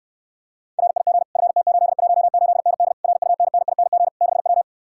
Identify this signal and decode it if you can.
Morse code, heard as a keyed tone, about 69 wpm, 690 Hz. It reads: RIG FT818ND RUNNING 6W